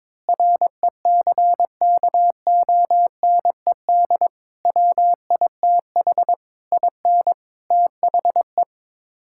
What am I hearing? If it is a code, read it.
Morse code, heard as a keyed tone, about 22 wpm, 700 Hz. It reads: RECKONED WITH IN THE
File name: practice / 22wpm / Sets of 8 Words - 22wpm